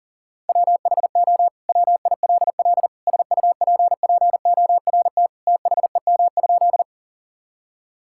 Morse code, heard as a keyed tone, about 40 wpm, 700 Hz. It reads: WHO WILL SUPPORT THEM?
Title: Sentences from Top 1000 Words - Review Part 5 - 40wpm